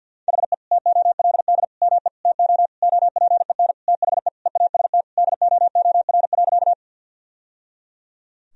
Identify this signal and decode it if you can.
Morse code, heard as a keyed tone, about 50 wpm, 700 Hz. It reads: HE TOLD ME TO OPEN THE EAST DOOR.
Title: Sentences from Top 1000 Words - Review Part 3 - 50wpm